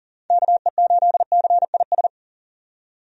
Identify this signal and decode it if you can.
Morse code, heard as a keyed tone, about 40 wpm, 700 Hz. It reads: KE8CIS
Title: USA Call Signs - 6 Characters - 40wpm